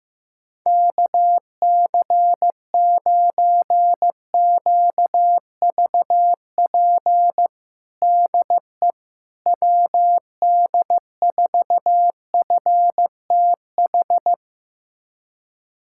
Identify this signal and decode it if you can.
Morse code, heard as a keyed tone, about 15 wpm, 700 Hz. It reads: KC9QVP DE WD4FTH
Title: USA Callsign DE Callsign - 6 Characters - 15wpm